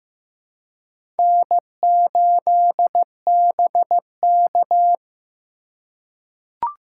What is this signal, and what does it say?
Morse code, heard as a keyed tone, about 15 wpm, 700 Hz. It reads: N8BK
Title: USA Call Signs - 4 Characters - 6x Character Spacing - 15wpm